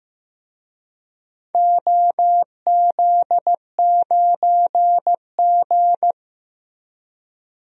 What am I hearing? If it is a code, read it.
Morse code, heard as a keyed tone, about 15 wpm, 700 Hz. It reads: OZ9G
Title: DXCC-100 Call Signs - 4 Characters - 15wpm